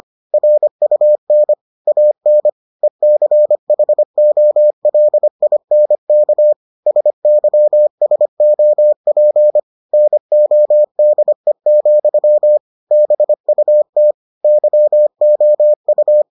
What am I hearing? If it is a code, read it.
Morse code, heard as a keyed tone, about 25 wpm, 600 Hz. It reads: RUN AN ECHOLINK SYSOP NODE, BUT YOU